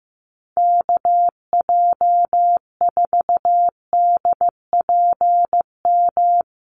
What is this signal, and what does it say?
Morse code, heard as a keyed tone, about 15 wpm, 700 Hz. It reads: KJ4DPM
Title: USA Call Signs Repeated 2x 15wpm